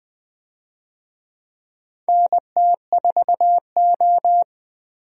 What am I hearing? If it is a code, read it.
Morse code, heard as a keyed tone, about 20 wpm, 700 Hz. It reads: NT4O
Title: USA Call Signs - 4 Characters - Repeated 2x - 20wpm